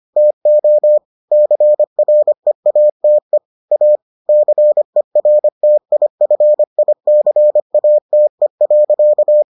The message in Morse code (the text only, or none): TO CREATE A CERTIFICATE.